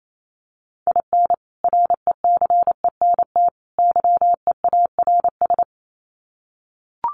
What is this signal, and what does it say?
Morse code, heard as a keyed tone, about 28 wpm, 700 Hz. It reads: IN RECENT YEARS
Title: Sets of 3 Words 28wpm